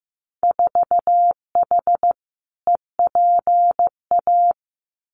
Morse code, heard as a keyed tone, about 16 wpm, 700 Hz. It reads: 4H EPA
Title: Winter Field Day Exchanges 15wpm